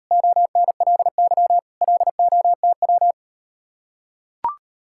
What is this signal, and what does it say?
Morse code, heard as a keyed tone, about 38 wpm, 700 Hz. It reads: ONLY LOTW